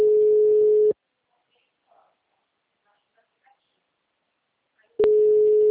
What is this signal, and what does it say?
Ringback tone (Europe, 425 Hz), with speech.
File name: out- -2013- - - .60422